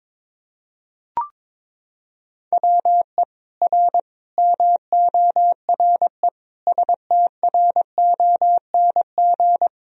Morse code, heard as a keyed tone, about 21 wpm, 700 Hz. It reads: WE R MORE STRONG
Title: Sentences from Top 500 Words and Abbreviations 22wpm